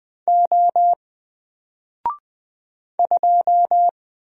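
Morse code, heard as a keyed tone, about 20 wpm, 700 Hz. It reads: O 2
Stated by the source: Single letter and number Morse drill at 20 wpm, from N4LCD.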